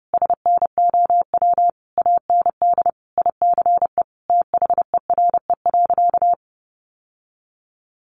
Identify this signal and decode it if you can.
Morse code, heard as a keyed tone, about 30 wpm, 700 Hz. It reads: SNOW AND ICE THERE.
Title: Sentences from Top 1000 Words 30wpm